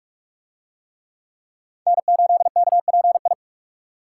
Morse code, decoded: N8KPI

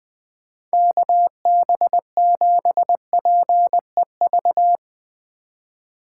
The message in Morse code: KB7PEV

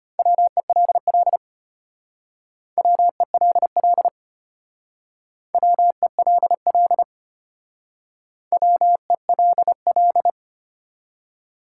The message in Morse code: WELL WELL WELL WELL

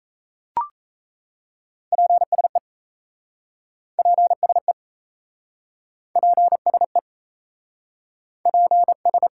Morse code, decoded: PSE PSE PSE PS

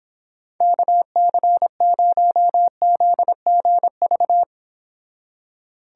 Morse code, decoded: KC0ZGV

